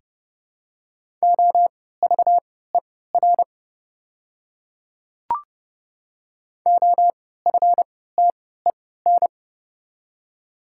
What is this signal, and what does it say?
Morse code, heard as a keyed tone, about 31 wpm, spaced as for 17 wpm, 700 Hz. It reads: OVER OFTEN
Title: 3x Character Spacing - Top 500 Words - 30wpm